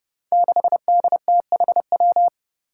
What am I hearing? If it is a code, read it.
Morse code, heard as a keyed tone, about 30 wpm, 700 Hz. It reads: 6DTHW
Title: Five Challenging Letter-Number Combinations 30wpm